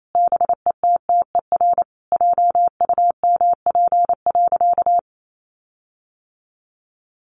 Morse code, heard as a keyed tone, about 28 wpm, 700 Hz. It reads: BETTER JUMP.
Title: Sentences from Top 1300 Words 28wpm